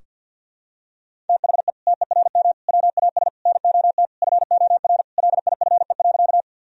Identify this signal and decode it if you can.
Morse code, heard as a keyed tone, about 46 wpm, 700 Hz. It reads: THE TERM WAS NOT FOR LIFE.
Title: Sentences from Top 1000 Words - Review Part 4 - 50wpm